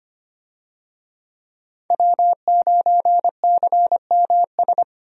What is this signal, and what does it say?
Morse code, heard as a keyed tone, about 25 wpm, 700 Hz. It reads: W9CMS
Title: USA Call Signs - 5 Characters - 25wpm